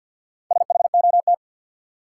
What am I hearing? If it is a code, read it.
Morse code, heard as a keyed tone, about 52 wpm, 700 Hz. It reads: SHOT
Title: Sentences from Top 1100 Words - New Word Drill - 50wpm